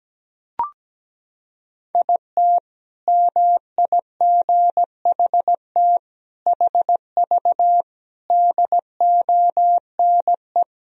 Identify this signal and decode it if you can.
Morse code, heard as a keyed tone, about 17 wpm, 700 Hz. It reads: IT MIGHT HV DONE
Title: Sentences from Top 500 Words and Abbreviations 17wpm